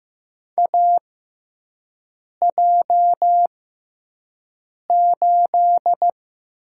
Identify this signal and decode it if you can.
Morse code, heard as a keyed tone, about 15 wpm, spaced as for 5 wpm, 700 Hz. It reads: AJ8
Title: USA Call Signs - 6x Character Spacing - 15wpm